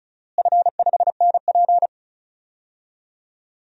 Morse code, heard as a keyed tone, about 34 wpm, 700 Hz. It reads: F5NP
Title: DXCC-100 Call Signs - 4 Characters - Repeated 3x - 35wpm